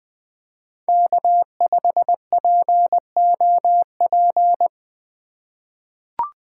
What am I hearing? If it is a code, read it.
Morse code, heard as a keyed tone, about 21 wpm, 700 Hz. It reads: K5POP
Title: USA Call Signs - 5 Characters - 2x Character Spacing - 20wpm